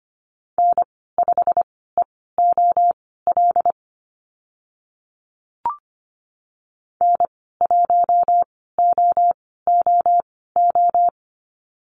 Morse code, heard as a keyed tone, about 25 wpm, spaced as for 16 wpm, 700 Hz. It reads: N5EOL N1OOO
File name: Lesson 15g - Number 5 - Call Signs 25wpm